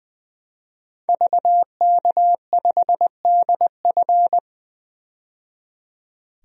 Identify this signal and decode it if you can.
Morse code, heard as a keyed tone, about 20 wpm, 700 Hz. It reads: VK5DF
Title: DXCC-100 Call Signs - 5 Characters - 20wpm